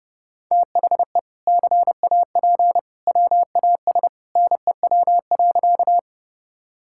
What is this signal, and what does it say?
Morse code, heard as a keyed tone, about 32 wpm, 700 Hz. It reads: THE CAP WAS NEW.
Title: Sentences from Top 1600 Words 30wpm